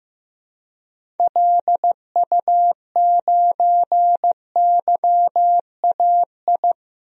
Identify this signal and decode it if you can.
Morse code, heard as a keyed tone, about 15 wpm, 700 Hz. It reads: LU9YAI
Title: DXCC-100 Call Signs - 6 Characters - 15wpm